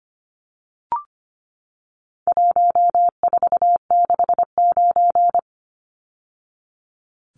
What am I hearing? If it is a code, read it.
Morse code, heard as a keyed tone, about 25 wpm, 700 Hz. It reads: 1469
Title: Numbers - 4-Digit 25wpm